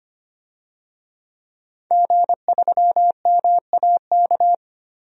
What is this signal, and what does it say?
Morse code, heard as a keyed tone, about 25 wpm, 700 Hz. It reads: G3MAK
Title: DXCC-100 Call Signs - 5 Characters - Repeated 2x - 25wpm